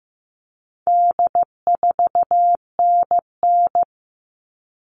Morse code, heard as a keyed tone, about 15 wpm, 700 Hz. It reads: D4NN
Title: All Cut Numbers - 4-Digit 15wpm